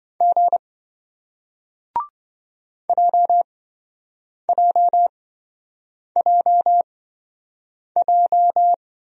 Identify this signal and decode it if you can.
Morse code, heard as a keyed tone, about 27 wpm, spaced as for 7 wpm, 700 Hz. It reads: G JJJJ